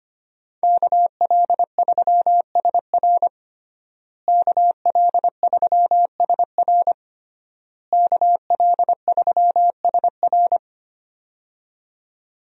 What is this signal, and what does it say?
Morse code, heard as a keyed tone, about 25 wpm, 700 Hz. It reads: KL3SR KL3SR KL3SR
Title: USA Call Signs - 5 Characters - Repeated 3x - 25wpm